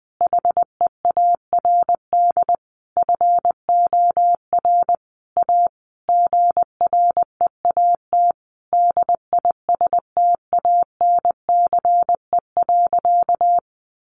Morse code, heard as a keyed tone, about 20 wpm, 700 Hz. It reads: HEARD FOR A GREAT DISTANCE.
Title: Sentences from Top 1200 Words 20wpm